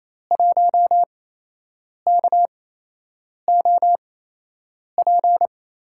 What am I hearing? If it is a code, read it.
Morse code, heard as a keyed tone, about 28 wpm, spaced as for 8 wpm, 700 Hz. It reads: 1KOP
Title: USA Call Signs - 8x Character Spacing - 28wpm